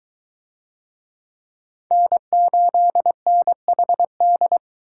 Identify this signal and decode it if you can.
Morse code, heard as a keyed tone, about 22 wpm, 700 Hz. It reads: N8NHD